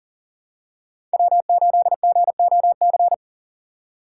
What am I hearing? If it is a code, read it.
Morse code, heard as a keyed tone, about 40 wpm, 700 Hz. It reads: W8GOC
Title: USA Call Signs - 5 Characters - 3x Character Spacing - 40wpm